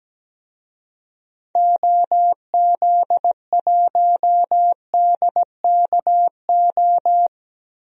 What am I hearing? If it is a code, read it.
Morse code, heard as a keyed tone, about 17 wpm, 700 Hz. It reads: OZ1DKO
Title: DXCC-100 Call Signs - 6 Characters - 17wpm